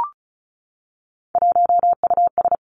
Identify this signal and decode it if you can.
Morse code, heard as a keyed tone, about 36 wpm, 700 Hz. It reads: 1US
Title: Three Challenging Letter-Number Combinations 35wpm